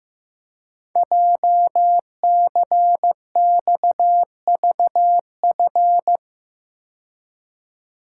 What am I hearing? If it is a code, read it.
Morse code, heard as a keyed tone, about 15 wpm, 700 Hz. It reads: JCXVF
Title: Code Groups - 5 Letters - 15wpm